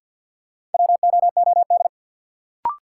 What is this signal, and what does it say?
Morse code, heard as a keyed tone, about 48 wpm, 700 Hz. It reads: WOOD
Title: Sentences from Top 1000 Words - Review Part 2 - New Word Drill - 50wpm